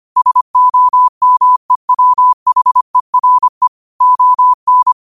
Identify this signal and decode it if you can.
Morse code, heard as a keyed tone, about 25 wpm, 1000 Hz. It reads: SOMEWHERE ON
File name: qotdfiles / 25wpm